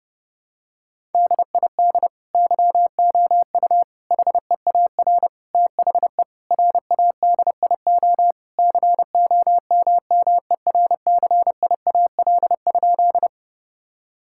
Morse code, heard as a keyed tone, about 30 wpm, 700 Hz. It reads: DID YOU HEAR THE RADIO COMMERCIAL?
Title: Sentences from Top 1000 Words - Review Part 5 - 30wpm